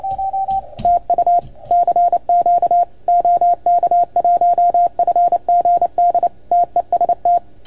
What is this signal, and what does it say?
Morse code, heard as a keyed tone, about 28 wpm, 690 Hz. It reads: TU CQ OK1FGD TEST
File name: Archiv / CUC310 / OK1FGD CUC 310 pileup